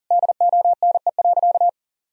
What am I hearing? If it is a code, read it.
Morse code, heard as a keyed tone, about 40 wpm, 700 Hz. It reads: DONE.
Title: Sentences from Top 1000 Words - Review Part 1 - 40wpm